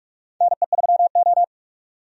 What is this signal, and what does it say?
Morse code, heard as a keyed tone, about 41 wpm, 700 Hz. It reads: NE3O
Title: USA Call Signs - 4 Characters - 45wpm